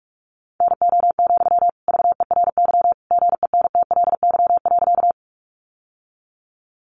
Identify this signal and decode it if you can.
Morse code, heard as a keyed tone, about 44 wpm, 700 Hz. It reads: NO, VERY GENTLY.
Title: Sentences from Top 1500 Words 45wpm